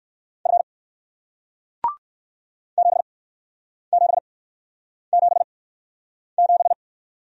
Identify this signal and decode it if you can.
Morse code, heard as a keyed tone, about 57 wpm, spaced as for 9 wpm, 700 Hz. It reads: F 7777